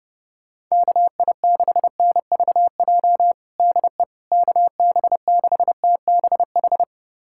Morse code, heard as a keyed tone, about 30 wpm, 700 Hz. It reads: KI6NVJ DE KB6TBH
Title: USA Callsign DE Callsign - 6 Characters - 30wpm